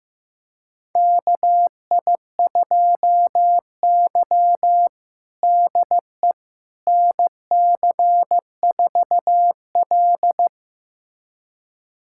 Morse code, heard as a keyed tone, about 15 wpm, 700 Hz. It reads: KI2Y DE NC4L
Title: USA Callsign DE Callsign - 4 Characters - 15wpm